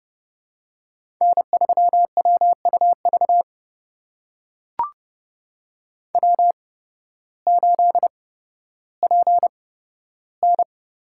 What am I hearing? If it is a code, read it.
Morse code, heard as a keyed tone, about 31 wpm, spaced as for 14 wpm, 700 Hz. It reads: N3WUV W8PN
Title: USA Call Signs - 5 Characters - 8x Character Spacing - 30wpm